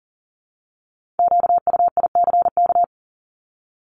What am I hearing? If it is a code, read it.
Morse code, heard as a keyed tone, about 39 wpm, 700 Hz. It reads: QUICK
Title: Multi-letter Phonemes 40wpm